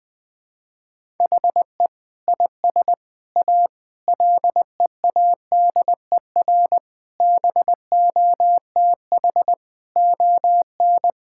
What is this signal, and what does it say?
Morse code, heard as a keyed tone, about 21 wpm, 700 Hz. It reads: HE IS A LEADER BOTH ON
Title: Sentences with 6-Word Phrases - 20wpm